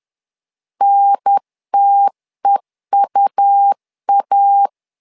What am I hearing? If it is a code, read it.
Morse code, heard as a keyed tone, about 10 wpm, 800 Hz. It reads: NTEUA
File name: LEZIONI / 2 UHDN / 2 UHDN 10wpm 1904